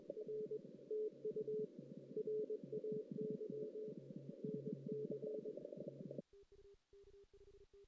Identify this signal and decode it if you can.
Morse code, heard as a keyed tone, about 21 wpm, 410 Hz. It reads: R TU RAY NC
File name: 2016 JAN CW NAQP